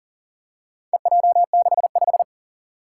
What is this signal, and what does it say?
Morse code, heard as a keyed tone, about 40 wpm, 700 Hz. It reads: EJ65
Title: Four Challenging Letter-Number Combinations 40wpm